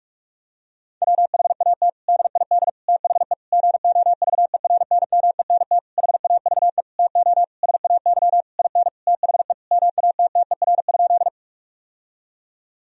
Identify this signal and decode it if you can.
Morse code, heard as a keyed tone, about 46 wpm, 700 Hz. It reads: WHAT DID THE GOVERNMENT HAVE TO SAY IN THE MATTER?
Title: Sentences from Top 1000 Words - Review Part 3 - 45wpm